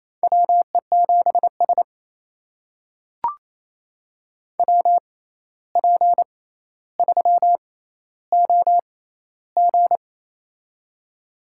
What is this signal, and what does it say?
Morse code, heard as a keyed tone, about 28 wpm, spaced as for 13 wpm, 700 Hz. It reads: WE7S WP3OG
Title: USA Call Signs - 6x Character Spacing - 28wpm